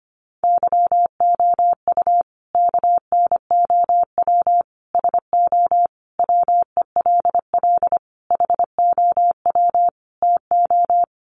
Morse code, heard as a keyed tone, about 25 wpm, 700 Hz. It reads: YOU KNOW SO WELL HOW TO
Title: Sentences from Top 800 Words 25wpm